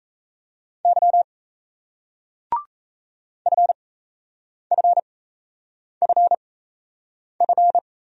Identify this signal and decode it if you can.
Morse code, heard as a keyed tone, about 36 wpm, spaced as for 8 wpm, 700 Hz. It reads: Y FFFF